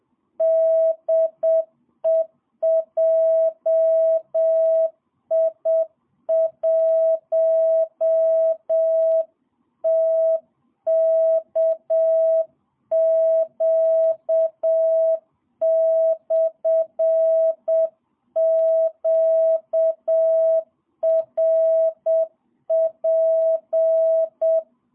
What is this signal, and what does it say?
Morse code, heard as a keyed tone, about 7 wpm, 650 Hz. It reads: DE JI1TKQ/QRP
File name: jh8cbh / 2022cw